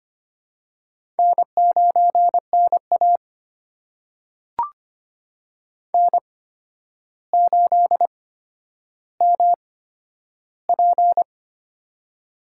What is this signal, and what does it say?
Morse code, heard as a keyed tone, about 25 wpm, spaced as for 11 wpm, 700 Hz. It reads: N9NA N8MP
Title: USA Call Signs - 4 Characters - 8x Character Spacing - 25wpm